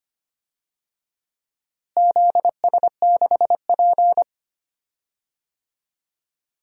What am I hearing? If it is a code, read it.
Morse code, heard as a keyed tone, about 25 wpm, 700 Hz. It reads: ZS6P
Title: DXCC-100 Call Signs - 4 Characters - Repeated 3x - 25wpm